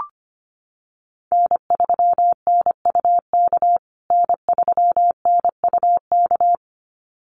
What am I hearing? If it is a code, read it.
Morse code, heard as a keyed tone, about 25 wpm, 700 Hz. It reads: N3NUK N3NUK
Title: USA Call Signs Repeated 2x 25wpm